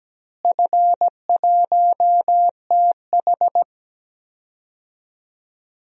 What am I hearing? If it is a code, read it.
Morse code, heard as a keyed tone, about 17 wpm, 700 Hz. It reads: F1TH
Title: DXCC-100 Call Signs - 4 Characters - Repeated 3x - 17wpm